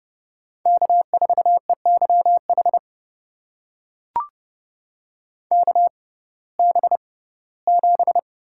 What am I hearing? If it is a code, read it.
Morse code, heard as a keyed tone, about 30 wpm, spaced as for 18 wpm, 700 Hz. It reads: K4EYH KB7
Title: USA Call Signs - 5 Characters - 6x Character Spacing - 30wpm